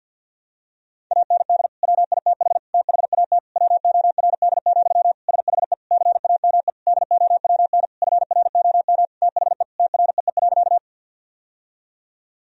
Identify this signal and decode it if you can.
Morse code, heard as a keyed tone, about 50 wpm, 700 Hz. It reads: AND WITH THAT WORD, SHE CAME DOWN FROM THE TREE.
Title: Sentences from Top 1000 Words - Review Part 1 - 50wpm